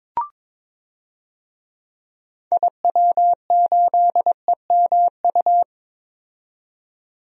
Morse code, heard as a keyed tone, about 22 wpm, 700 Hz. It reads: IW8EMU
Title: DXCC-100 Call Signs - 6 Characters - 22wpm